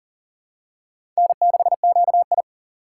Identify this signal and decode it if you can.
Morse code, heard as a keyed tone, about 40 wpm, 700 Hz. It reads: N6QI
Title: USA Call Signs - 4 Characters - 4x Character Spacing - 40wpm